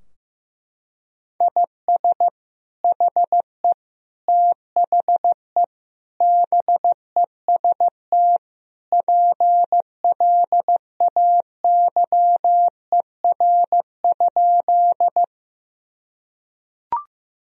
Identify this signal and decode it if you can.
Morse code, heard as a keyed tone, about 15 wpm, 700 Hz. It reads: IS HE THE BEST PLAYER?
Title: Sentences from Top 1000 Words - Review Part 4 - 15wpm